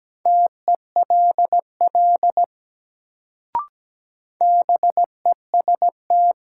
Morse code, heard as a keyed tone, about 17 wpm, 700 Hz. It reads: TELL BEST
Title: Sets of 2 Words - New Word Drill - 17wpm